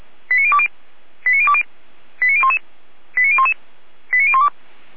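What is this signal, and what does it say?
Five-tone selective call (ZVEI), heard as quick runs of five tones, tones of about 80 ms. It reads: 89020 89020 89011 89011 89012
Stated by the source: ZVEI five-tone selective call received via SDR, fire brigade Feuerwehr Orlamünde.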